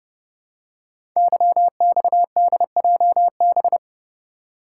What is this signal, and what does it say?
Morse code, heard as a keyed tone, about 29 wpm, 700 Hz. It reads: YXDJB
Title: Code Groups - 5 Letters - 30wpm